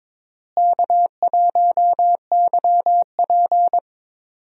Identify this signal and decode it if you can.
Morse code, heard as a keyed tone, about 21 wpm, 700 Hz. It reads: K1YP
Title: USA Call Signs - 4 Characters - Repeated 3x - 22wpm